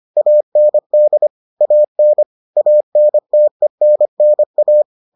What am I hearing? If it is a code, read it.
Morse code, heard as a keyed tone, about 25 wpm, 600 Hz. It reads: AND AN ANTENNA